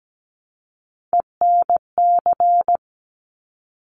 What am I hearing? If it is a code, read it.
Morse code, heard as a keyed tone, about 17 wpm, 700 Hz. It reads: ENC
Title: Top Three Letter Combinations in Words 17wpm